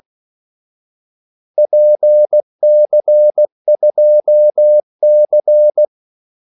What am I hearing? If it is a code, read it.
Morse code, heard as a keyed tone, about 16 wpm, 600 Hz. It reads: PC2C
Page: Kaldesignaler 16 wpm